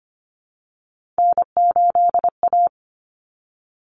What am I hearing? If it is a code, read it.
Morse code, heard as a keyed tone, about 25 wpm, 700 Hz. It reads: N8A
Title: Common Cut Numbers - 3-Digit 25wpm